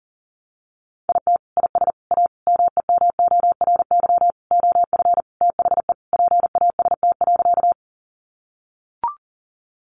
Morse code, heard as a keyed tone, about 40 wpm, 700 Hz. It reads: IT IS A MEMORY OF THE PAST.
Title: Sentences from Top 1300 Words 40wpm